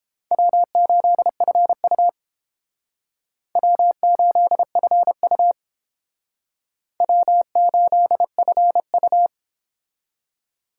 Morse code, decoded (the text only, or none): W8FU W8FU W8FU